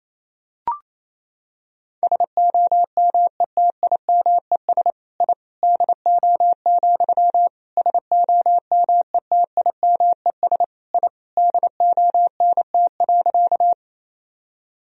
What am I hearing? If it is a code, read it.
Morse code, heard as a keyed tone, about 28 wpm, 700 Hz. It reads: SOMETIMES I DO, SOMETIMES I DONT.
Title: Sentences from Top 1000 Words - Review Part 3 - 28wpm